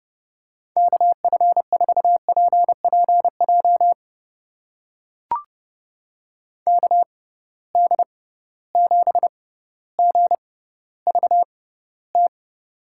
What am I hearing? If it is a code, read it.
Morse code, heard as a keyed tone, about 30 wpm, spaced as for 15 wpm, 700 Hz. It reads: KF4PPJ KD7GVT
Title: USA Call Signs - 6x Character Spacing - 30wpm